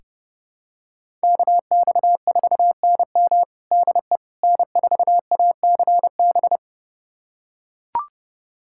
Morse code, heard as a keyed tone, about 30 wpm, 700 Hz. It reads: KX4NM DE N4ACB
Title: USA Callsign DE Callsign 30wpm